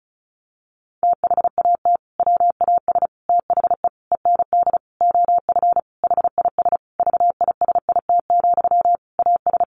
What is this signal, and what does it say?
Morse code, heard as a keyed tone, about 34 wpm, 700 Hz. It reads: THAT WAS THE END OF HIS VISIT, AS
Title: Sentences from Top 1400 Words 35wpm